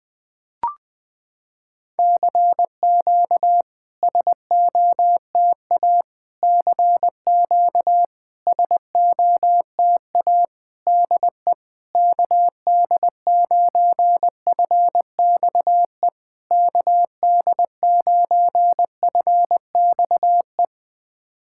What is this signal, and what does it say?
Morse code, heard as a keyed tone, about 20 wpm, 700 Hz. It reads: CQ SOTA CQ SOTA DE KD9FXE KD9FXE
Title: SOTA QSO Practice 20wpm